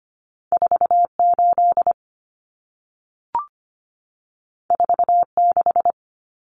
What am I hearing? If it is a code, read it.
Morse code, heard as a keyed tone, about 25 wpm, 700 Hz. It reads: 48 46
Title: Common Cut Numbers - 2-Digit 25wpm